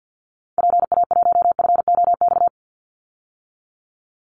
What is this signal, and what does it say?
Morse code, heard as a keyed tone, about 50 wpm, 700 Hz. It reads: PA1FOX
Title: Call Signs - 50wpm